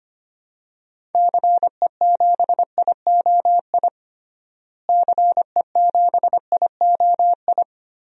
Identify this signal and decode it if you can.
Morse code, heard as a keyed tone, about 25 wpm, 700 Hz. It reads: CE7IOI CE7IOI
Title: DXCC-100 Call Signs - 6 Characters - Repeated 3x - 25wpm